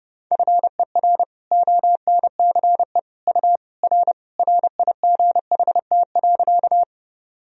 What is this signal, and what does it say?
Morse code, heard as a keyed tone, about 30 wpm, 700 Hz. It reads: FER ONCE U R RIGHT.
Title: Sentences from Top 300 Words and Abbreviations 30wpm